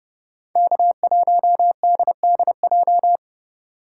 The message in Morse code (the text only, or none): K1DDJ